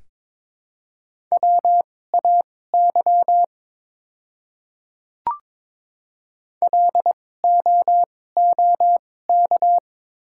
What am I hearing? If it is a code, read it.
Morse code, heard as a keyed tone, about 21 wpm, spaced as for 16 wpm, 700 Hz. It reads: WAY LOOK
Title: 2x Character Spacing - Top 100 Words - 22wpm